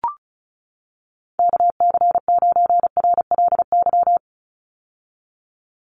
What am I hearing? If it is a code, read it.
Morse code, heard as a keyed tone, about 34 wpm, 700 Hz. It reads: KC9RLY
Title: USA Call Signs 35wpm